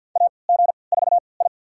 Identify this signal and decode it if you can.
Morse code, heard as a keyed tone, about 50 wpm, spaced as for 28 wpm, 700 Hz. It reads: AG4I
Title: USA Call Signs - 4 Characters - 3x Character Spacing - 50wpm